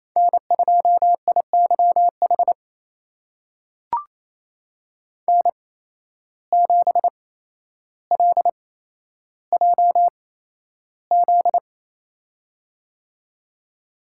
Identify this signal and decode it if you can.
Morse code, heard as a keyed tone, about 28 wpm, spaced as for 12 wpm, 700 Hz. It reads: N2IYH N7LJZ
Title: USA Call Signs - 5 Characters - 8x Character Spacing - 28wpm